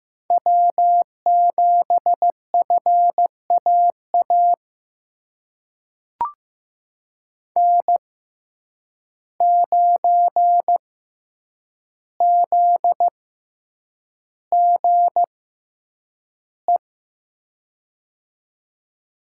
Morse code, heard as a keyed tone, about 15 wpm, spaced as for 8 wpm, 700 Hz. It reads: W7FAA N9ZGE
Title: USA Call Signs - 5 Characters - 6x Character Spacing - 15wpm